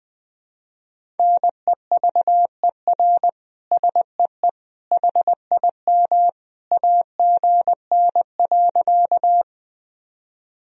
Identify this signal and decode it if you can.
Morse code, heard as a keyed tone, about 21 wpm, 700 Hz. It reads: NEVER SEE HIM AGN.
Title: Sentences from Top 300 Words and Abbreviations 20wpm